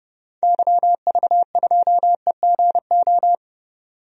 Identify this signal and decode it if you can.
Morse code, heard as a keyed tone, about 30 wpm, 700 Hz. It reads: YV2EGO
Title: DXCC-100 Call Signs - 6 Characters - 30wpm